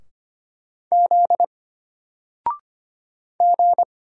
Morse code, heard as a keyed tone, about 25 wpm, 700 Hz. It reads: Z G
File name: Single-Letter-Number-25wpm